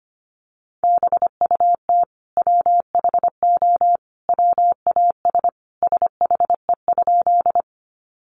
Morse code, heard as a keyed tone, about 25 wpm, 700 Hz. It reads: BUT WHO WAS SHE?
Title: Sentences from Top 100 Words 25wpm